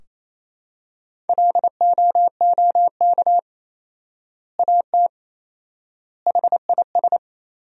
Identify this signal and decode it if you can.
Morse code, heard as a keyed tone, about 27 wpm, 700 Hz. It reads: LOOK AT HIS
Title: 4x Word-Spacing - Sentences from Top 300 Words - 28wpm